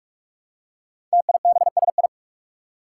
Morse code, heard as a keyed tone, about 44 wpm, 700 Hz. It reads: TIBSI
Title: Five Challenging Letter-Number Combinations 45wpm